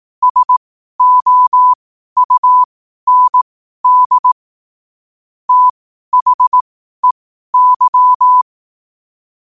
Morse code, heard as a keyed tone, about 18 wpm, spaced as for 12 wpm, 1000 Hz. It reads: SOUND THEY
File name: qotdfiles / 12wpm